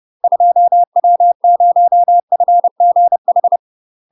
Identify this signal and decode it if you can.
Morse code, heard as a keyed tone, about 30 wpm, 700 Hz. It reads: 2W0FGH
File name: UK MASTER QSOs TRAINING / 30WPM / UK MASTER QSOs 0028